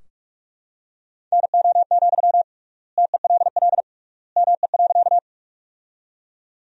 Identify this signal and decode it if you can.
Morse code, heard as a keyed tone, about 45 wpm, 700 Hz. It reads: NO, TELL ME.
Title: 3x Word-Spacing - Sentences from Top 200 Words - 45wpm